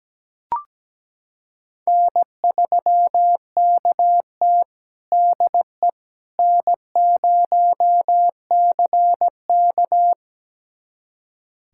Morse code, heard as a keyed tone, about 17 wpm, 700 Hz. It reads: N3KT DE N0CK
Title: USA Callsign DE Callsign - 4 Characters - 17wpm